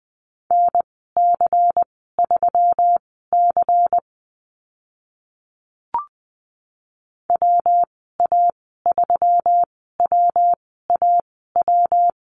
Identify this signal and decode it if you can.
Morse code, heard as a keyed tone, about 20 wpm, spaced as for 14 wpm, 700 Hz. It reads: NC3C WA3WAW
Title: 140 Lesson 23g - Letter f - Call Signs 20wpm